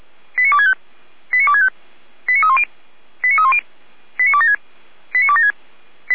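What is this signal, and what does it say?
Five-tone selective call (ZVEI), heard as quick runs of five tones, tones of about 70 ms. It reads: 89265 89265 89310 89310 89276 89276
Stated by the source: ZVEI five-tone call received by SDR, from the Feuerwehr Orlamünde server.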